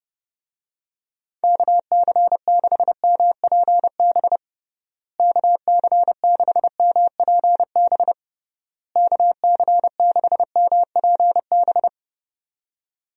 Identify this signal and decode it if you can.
Morse code, heard as a keyed tone, about 30 wpm, 700 Hz. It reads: KC6MPB KC6MPB KC6MPB
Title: USA Call Signs - 6 Characters - Repeated 3x - 30wpm